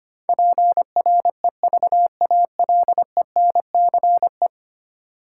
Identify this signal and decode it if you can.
Morse code, heard as a keyed tone, about 25 wpm, 700 Hz. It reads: PREVALENCE
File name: Ten Letter Words - 25wpm